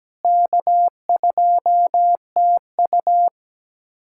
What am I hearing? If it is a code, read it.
Morse code, heard as a keyed tone, about 17 wpm, 700 Hz. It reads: K2TU
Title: USA Callsign DE Callsign - 4 Characters - 17wpm